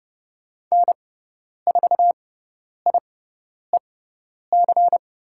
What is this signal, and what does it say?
Morse code, heard as a keyed tone, about 30 wpm, spaced as for 10 wpm, 700 Hz. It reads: N4IEC
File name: Lesson 17g - Letter u - Call Signs 30wpm